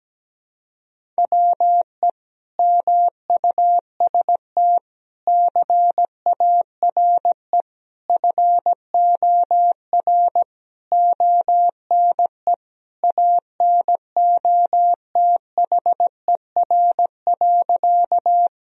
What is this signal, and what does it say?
Morse code, heard as a keyed tone, about 17 wpm, 700 Hz. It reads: WE MUST CARE FOR ONE ANOTHER.
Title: Sentences from Top 1000 Words - Review Part 1 - 17wpm